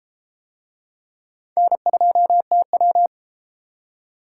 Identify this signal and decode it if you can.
Morse code, heard as a keyed tone, about 33 wpm, 700 Hz. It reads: N2TW